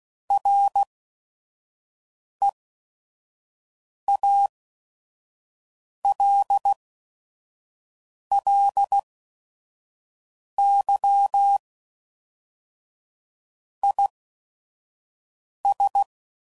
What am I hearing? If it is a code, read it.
Morse code, heard as a keyed tone, about 16 wpm, spaced as for 5 wpm, 800 Hz. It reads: REALLY IS
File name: CW Code Groups @5WPM / Technical Subjects / Gates Intro